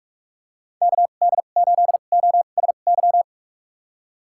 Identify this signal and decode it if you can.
Morse code, heard as a keyed tone, about 45 wpm, spaced as for 32 wpm, 690 Hz. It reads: KD8OSY
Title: USA Call Signs - 6 Characters - 2x Character Spacing - 45wpm